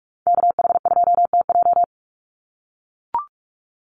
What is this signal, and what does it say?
Morse code, heard as a keyed tone, about 44 wpm, 700 Hz. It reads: KH2TJ
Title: NA SOTA Activator Callsigns 45wpm